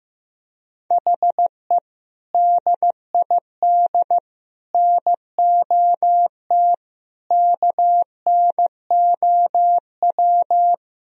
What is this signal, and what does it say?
Morse code, heard as a keyed tone, about 16 wpm, 700 Hz. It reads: HE DID NOT KNOW
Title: Sentences from Top 1000 Words - Review Part 1 - 15wpm